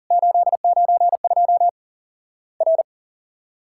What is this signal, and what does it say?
Morse code, heard as a keyed tone, about 40 wpm, 700 Hz. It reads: 892 R
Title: Worked All Europe DX Contest - QTC Exchange - 40wpm